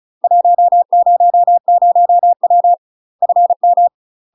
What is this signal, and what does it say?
Morse code, heard as a keyed tone, about 35 wpm, 700 Hz. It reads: 100W FM
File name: UK MASTER QSOs TRAINING / 35WPM / UK MASTER QSOs 0017